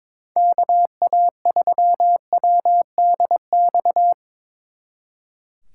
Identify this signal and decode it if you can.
Morse code, heard as a keyed tone, about 25 wpm, 700 Hz. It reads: KA3WDX
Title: Speed Racing - USA 2x3 Call Signs 22wpm